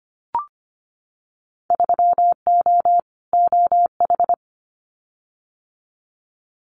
Morse code, heard as a keyed tone, about 25 wpm, 700 Hz. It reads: 3O OH
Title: Winter Field Day Exchanges 25wpm